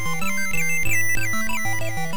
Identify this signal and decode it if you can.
There is no code.